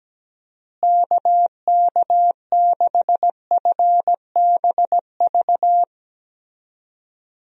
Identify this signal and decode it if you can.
Morse code, heard as a keyed tone, about 17 wpm, 700 Hz. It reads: KK6FBV
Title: USA Call Signs - 6 Characters - Repeated 3x - 17wpm